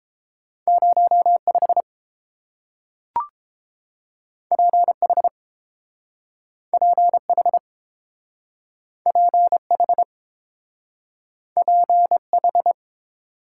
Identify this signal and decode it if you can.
Morse code, heard as a keyed tone, about 30 wpm, 700 Hz. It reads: 0 5 P H P H P H P H